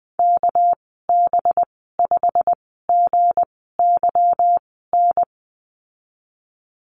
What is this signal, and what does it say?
Morse code, heard as a keyed tone, about 20 wpm, spaced as for 15 wpm, 700 Hz. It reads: KB5GYN
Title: 231 Lesson 37g - Letter x - Call Signs 20wpm